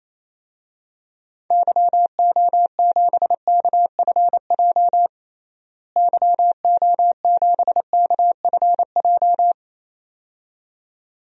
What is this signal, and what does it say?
Morse code, heard as a keyed tone, about 28 wpm, 700 Hz. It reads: YO7KFJ YO7KFJ
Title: DXCC-100 Call Signs - 6 Characters - Repeated 2x - 28wpm